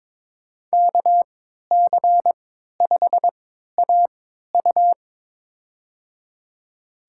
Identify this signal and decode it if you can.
Morse code, heard as a keyed tone, about 22 wpm, spaced as for 12 wpm, 700 Hz. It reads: KC5AU
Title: USA Call Signs - 5 Characters - 3x Character Spacing - 22wpm